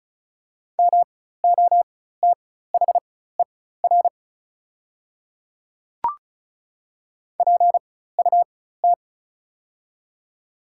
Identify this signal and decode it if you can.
Morse code, heard as a keyed tone, about 34 wpm, spaced as for 16 wpm, 690 Hz. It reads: MOTHER PUT
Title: 4x Character Spacing - Top 300 Words - 35wpm